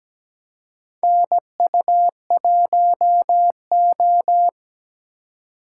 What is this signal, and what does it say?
Morse code, heard as a keyed tone, about 17 wpm, 700 Hz. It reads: NU1O
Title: K1USN SST QSO Practice - 17wpm